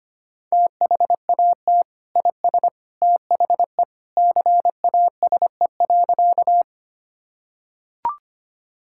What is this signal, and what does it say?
Morse code, heard as a keyed tone, about 25 wpm, 700 Hz. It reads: THAT IS THE CASE.
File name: practice / 25wpm / Sentences with 2-Word Phrases - 25wpm